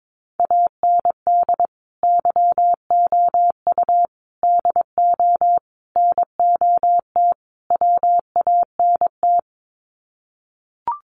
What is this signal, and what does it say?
Morse code, heard as a keyed tone, about 22 wpm, 700 Hz. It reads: AND YOU DO NOT WANT
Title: Sets of 5 Words Encore2 22wpm